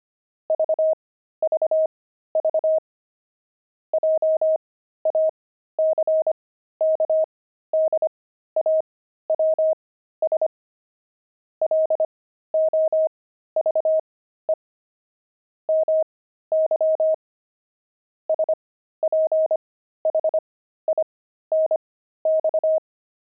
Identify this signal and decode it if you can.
Morse code, heard as a keyed tone, about 25 wpm, spaced as for 13 wpm, 630 Hz. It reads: VVV JACKDAWS LOVE MY SPHINX
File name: pangram-fs-13-cs-25-love